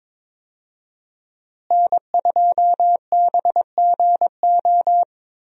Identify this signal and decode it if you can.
Morse code, heard as a keyed tone, about 22 wpm, 700 Hz. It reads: N2BGO
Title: USA Call Signs - 5 Characters - Repeated 3x - 22wpm